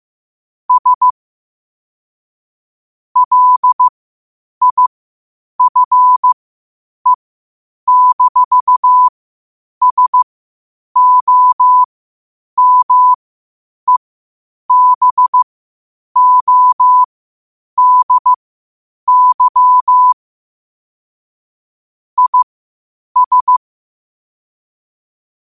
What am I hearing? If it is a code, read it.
Morse code, heard as a keyed tone, about 15 wpm, spaced as for 8 wpm, 1000 Hz. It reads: S LIFE-SOMEBODY IS
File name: qotdfiles / 8wpm